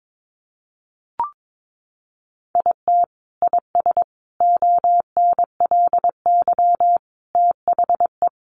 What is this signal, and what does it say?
Morse code, heard as a keyed tone, about 22 wpm, 700 Hz. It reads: IT IS ONLY THE
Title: Sentences from Top 800 Words 22wpm